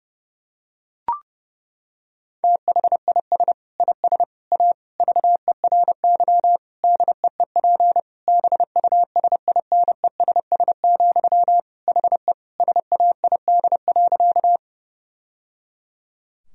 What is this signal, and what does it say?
Morse code, heard as a keyed tone, about 32 wpm, 700 Hz. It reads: THIS IS A VERY DEEP BUSINESS, HE SAID.
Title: Sentences from Top 1000 Words - Review Part 3 - 30wpm